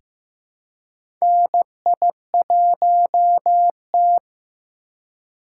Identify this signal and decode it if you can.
Morse code, heard as a keyed tone, about 15 wpm, 700 Hz. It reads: NI1T
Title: USA Call Signs - 4x Character Spacing - 15wpm